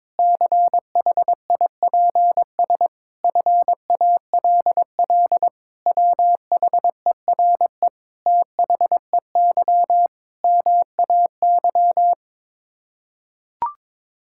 Morse code, heard as a keyed tone, about 22 wpm, 700 Hz. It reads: CHIPS FALL WHERE THEY MAY
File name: practice / 22wpm / Sets of 8 Words - 22wpm